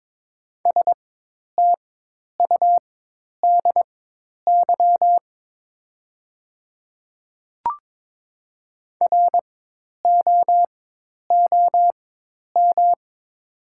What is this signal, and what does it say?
Morse code, heard as a keyed tone, about 22 wpm, spaced as for 10 wpm, 700 Hz. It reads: STUDY ROOM
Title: 4x Character Spacing - Top 300 Words - 22wpm